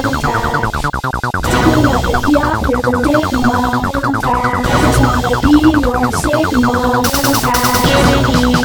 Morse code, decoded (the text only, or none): none